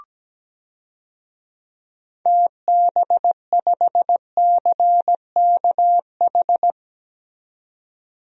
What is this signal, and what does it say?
Morse code, heard as a keyed tone, about 17 wpm, 700 Hz. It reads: TB5CKH
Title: DXCC-100 Call Signs - 6 Characters - 17wpm